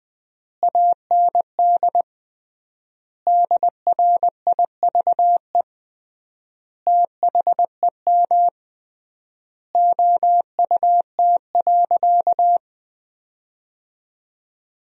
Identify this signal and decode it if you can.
Morse code, heard as a keyed tone, about 21 wpm, 700 Hz. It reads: AND DRIVE THEM OUT.
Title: 3x Word-Spacing - Sentences from Top 500 Words - 20wpm